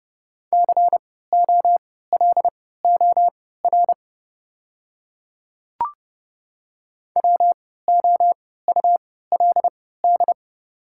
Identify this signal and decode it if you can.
Morse code, heard as a keyed tone, about 30 wpm, spaced as for 17 wpm, 690 Hz. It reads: COLOR WOULD
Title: 3x Character Spacing - Top 500 Words - 30wpm